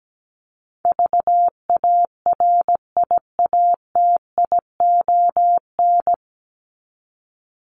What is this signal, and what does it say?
Morse code, heard as a keyed tone, about 17 wpm, 700 Hz. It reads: VARIATION
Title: Top 1501-1750 Words 17wpm